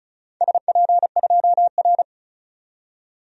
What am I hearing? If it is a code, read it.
Morse code, heard as a keyed tone, about 36 wpm, 700 Hz. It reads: SP2R
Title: DXCC-100 Call Signs - 4 Characters - 35wpm